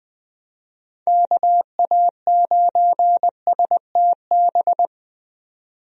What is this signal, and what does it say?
Morse code, heard as a keyed tone, about 20 wpm, 700 Hz. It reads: KA9STB
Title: USA Call Signs - 6 Characters - 20wpm